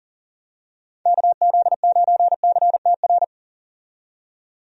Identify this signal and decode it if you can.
Morse code, heard as a keyed tone, about 39 wpm, 700 Hz. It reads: KZ9CTR
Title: USA Call Signs - 6 Characters - 40wpm